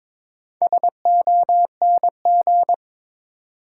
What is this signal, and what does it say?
Morse code, heard as a keyed tone, about 22 wpm, 700 Hz. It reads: SONG
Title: Sentences from Top 400 Words - New Word Drill - 22wpm